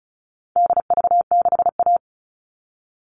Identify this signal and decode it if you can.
Morse code, heard as a keyed tone, about 34 wpm, 700 Hz. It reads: DV6A
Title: All Cut Numbers - 4-Digit 35wpm